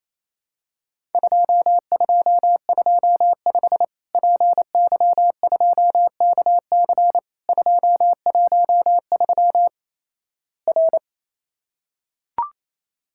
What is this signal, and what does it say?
Morse code, heard as a keyed tone, about 28 wpm, 700 Hz. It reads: 2225 PY2KC 213 R
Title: Worked All Europe DX Contest - QTC Exchange - 28wpm